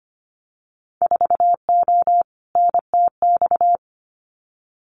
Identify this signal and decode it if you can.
Morse code, heard as a keyed tone, about 25 wpm, 700 Hz. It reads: 4O NTX
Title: Winter Field Day Exchanges 25wpm